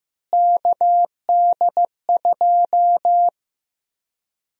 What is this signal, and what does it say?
Morse code, heard as a keyed tone, about 16 wpm, 700 Hz. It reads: KD2
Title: USA Call Sign Prefixes - Repeated 2x - 15wpm